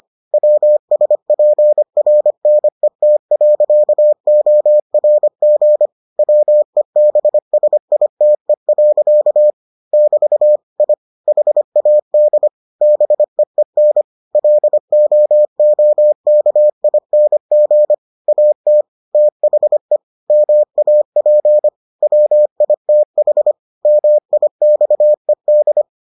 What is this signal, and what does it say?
Morse code, heard as a keyed tone, about 25 wpm, 600 Hz. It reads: WSPRNET.ORG WEBSITE. <BT> I HAD BEEN LOOKING AT THE MAP WITH MIXED